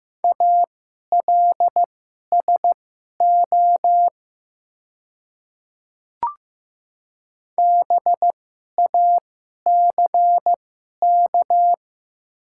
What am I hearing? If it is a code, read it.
Morse code, heard as a keyed tone, about 15 wpm, spaced as for 11 wpm, 700 Hz. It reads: ALSO BACK
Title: 2x Character Spacing - Top 200 Words - 15wpm